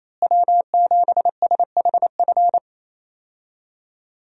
Morse code, heard as a keyed tone, about 28 wpm, 700 Hz. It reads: W7SHF
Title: USA Call Signs - 5 Characters - Repeated 2x - 28wpm